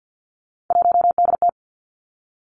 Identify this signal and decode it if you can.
Morse code, heard as a keyed tone, about 50 wpm, 700 Hz. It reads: 1DT
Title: Three Challenging Letter-Number Combinations 50wpm